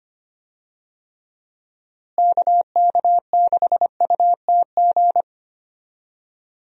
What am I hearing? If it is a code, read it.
Morse code, heard as a keyed tone, about 25 wpm, 700 Hz. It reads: KK6UTG
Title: USA Call Signs - 6 Characters - Repeated 3x - 25wpm